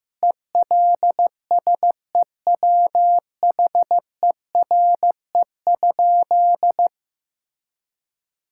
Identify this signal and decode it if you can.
Morse code, heard as a keyed tone, about 15 wpm, 700 Hz. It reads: ELSEWHERE?
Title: Sentences from Top 1600 Words 15wpm